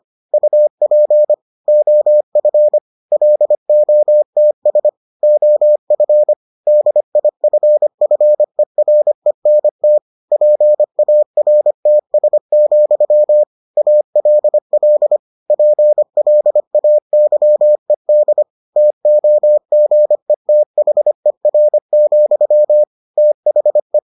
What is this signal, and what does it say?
Morse code, heard as a keyed tone, about 25 wpm, 600 Hz. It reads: UP OF LOTS OF DIFFERENT PARTS, ALL PLAYED TOGETHER, THE